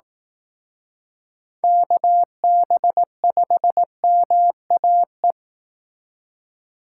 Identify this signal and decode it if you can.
Morse code, heard as a keyed tone, about 18 wpm, 700 Hz. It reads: KB5MAE
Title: Speed Racing - USA 2x3 Call Signs 15wpm